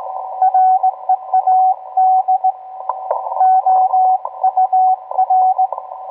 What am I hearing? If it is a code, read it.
Morse code, heard as a keyed tone, about 18 wpm, 750 Hz, with radio noise. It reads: READ OUR